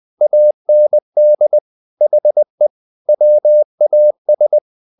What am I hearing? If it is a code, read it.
Morse code, heard as a keyed tone, about 20 wpm, 600 Hz. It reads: AND HE WAS
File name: the old man and the sea part 2 20wpm